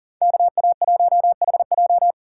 Morse code, heard as a keyed tone, about 50 wpm, 700 Hz. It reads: KA1HJ
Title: Speed Racing - USA 2x2 Call Signs 40wpm